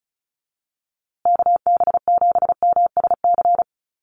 Morse code, heard as a keyed tone, about 34 wpm, 700 Hz. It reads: KB7MSC
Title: USA Call Signs Repeated 2x 35wpm